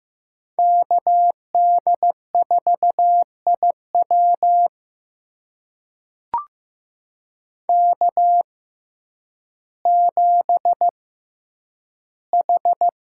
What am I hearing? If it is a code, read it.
Morse code, heard as a keyed tone, about 15 wpm, spaced as for 9 wpm, 700 Hz. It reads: KD4IW K7H
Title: USA Call Signs - 5 Characters - 6x Character Spacing - 15wpm